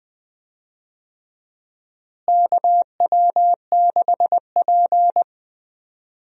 Morse code, heard as a keyed tone, about 20 wpm, 700 Hz. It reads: KW6P
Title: USA Call Signs - 4 Characters - Repeated 2x - 20wpm